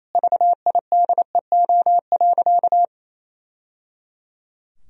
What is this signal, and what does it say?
Morse code, heard as a keyed tone, about 28 wpm, 700 Hz. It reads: VIDEO.
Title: Sentences from Top 1000 Words - Review Part 4 - 28wpm